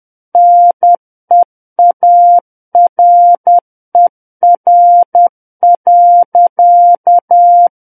Morse code, heard as a keyed tone, about 10 wpm, 700 Hz. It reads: NEARER.
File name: Book One / 01-Chapter 1 - 10 WPM